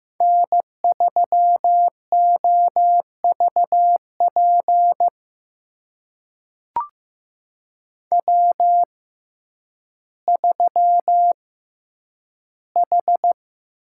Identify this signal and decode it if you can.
Morse code, heard as a keyed tone, about 15 wpm, spaced as for 9 wpm, 700 Hz. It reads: N3OVP W3H
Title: USA Call Signs - 6x Character Spacing - 15wpm